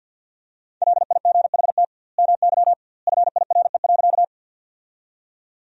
Morse code, heard as a keyed tone, about 50 wpm, 700 Hz. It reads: LIGHT MY FIRE.
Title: 2x Word-Spacing - Sentences from Top 400 Words - 50wpm